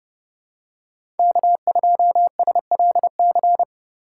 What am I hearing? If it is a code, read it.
Morse code, heard as a keyed tone, about 30 wpm, 700 Hz. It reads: K2SLC
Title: USA Call Signs - 5 Characters - 3x Character Spacing - 30wpm